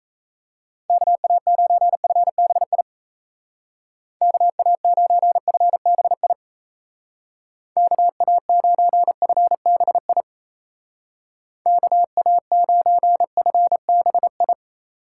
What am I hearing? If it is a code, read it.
Morse code, heard as a keyed tone, about 35 wpm, 700 Hz. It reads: KA9FBI KA9FBI KA9FBI KA9FBI